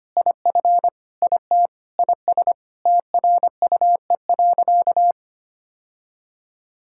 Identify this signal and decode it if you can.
Morse code, heard as a keyed tone, about 25 wpm, 700 Hz. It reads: IF IT IS TRUE.
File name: practice / 25wpm / Sentences with 4-Word Phrases - 25wpm